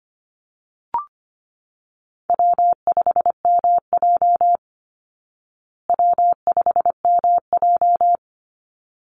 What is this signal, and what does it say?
Morse code, heard as a keyed tone, about 25 wpm, 700 Hz. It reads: W5MJ W5MJ
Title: Easy Call Signs Repeated 2x - 25wpm